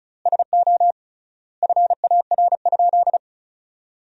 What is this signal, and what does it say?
Morse code, heard as a keyed tone, about 36 wpm, 700 Hz. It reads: SO FAR?
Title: 3x Word-Spacing - Sentences from Top 300 Words - 35wpm